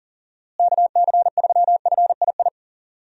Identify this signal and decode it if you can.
Morse code, heard as a keyed tone, about 40 wpm, 700 Hz. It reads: KC3FII
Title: USA Call Signs - 6 Characters - Repeated 2x - 40wpm